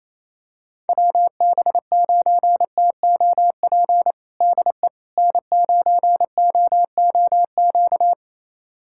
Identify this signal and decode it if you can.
Morse code, heard as a keyed tone, about 28 wpm, 700 Hz. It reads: WB9TOP DE N9OOQ
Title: USA Callsign DE Callsign 28wpm